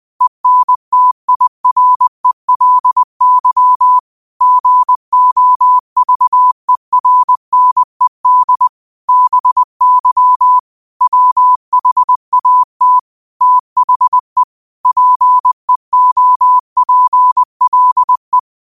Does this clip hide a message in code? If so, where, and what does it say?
Yes: Morse code from 0.2 to 18.4 s, heard as a keyed tone, about 20 wpm, 1000 Hz. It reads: ENTIRELY GOVERNED BY WHAT THE PEOPLE